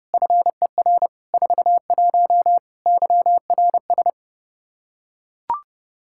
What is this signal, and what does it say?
Morse code, heard as a keyed tone, about 30 wpm, 700 Hz. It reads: FER 41 YRS
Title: K1USN SST QSO Practice 30wpm